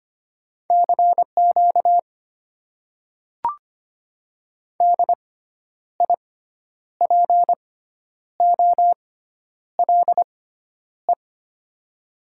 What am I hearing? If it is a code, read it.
Morse code, heard as a keyed tone, about 25 wpm, spaced as for 10 wpm, 700 Hz. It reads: CQ DIPOLE
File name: Top 90 QSO Elements - 6x Character Spacing - 25wpm